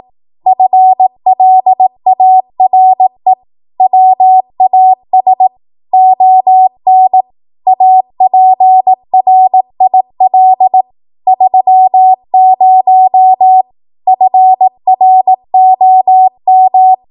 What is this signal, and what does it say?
Morse code, heard as a keyed tone, about 18 wpm, 750 Hz. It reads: FLARE WAS ON APRIL 30 FROM
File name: Morse / 18WPM